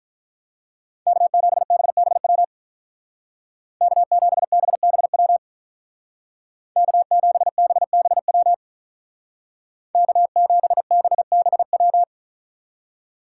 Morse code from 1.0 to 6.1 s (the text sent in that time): K7BBW K7BBW